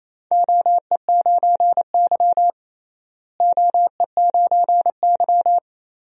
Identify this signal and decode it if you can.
Morse code, heard as a keyed tone, about 28 wpm, 700 Hz. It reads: OE9Y OE9Y
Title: DXCC-100 Call Signs - 4 Characters - Repeated 2x - 28wpm